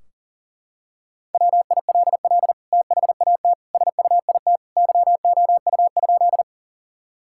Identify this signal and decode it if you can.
Morse code, heard as a keyed tone, about 40 wpm, 700 Hz. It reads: WILL THAT SUIT YOU?
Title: Sentences from Top 1600 Words 40wpm